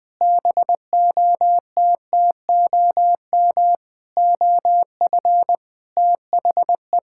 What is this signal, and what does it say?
Morse code, heard as a keyed tone, about 20 wpm, 700 Hz. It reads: BOTTOM OF THE
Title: Sentences with 7-Word Phrases - 20wpm